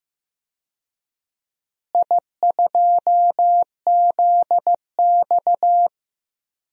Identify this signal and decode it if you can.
Morse code, heard as a keyed tone, about 15 wpm, 700 Hz. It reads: I2ZX
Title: DXCC-100 Call Signs - 4 Characters - Repeated 2x - 15wpm